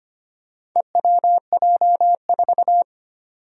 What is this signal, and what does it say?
Morse code, heard as a keyed tone, about 25 wpm, 700 Hz. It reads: EWJ4
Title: Four Challenging Letter-Number Combinations 25wpm